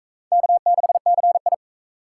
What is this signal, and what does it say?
Morse code, heard as a keyed tone, about 37 wpm, 700 Hz. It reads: K6CI